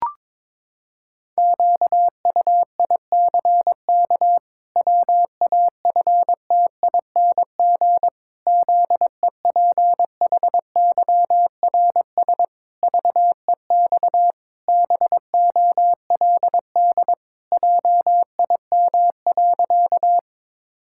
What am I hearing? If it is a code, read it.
Morse code, heard as a keyed tone, about 22 wpm, 700 Hz. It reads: QUICK WAFTING ZEPHYRS VEX BOLD JIM.
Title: Pangram Challenge v2 - 22wpm